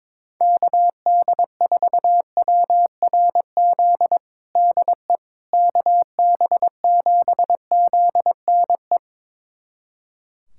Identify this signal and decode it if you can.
Morse code, heard as a keyed tone, about 22 wpm, 700 Hz. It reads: KD4WRZ DE KB7ZNE
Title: USA Callsign DE Callsign - 6 Characters - 22wpm